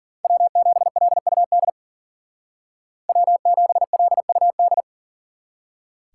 Morse code, heard as a keyed tone, about 49 wpm, 700 Hz. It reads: W7LUD W7LUD
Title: Speed Racing - USA 1x3 Call Signs 40wpm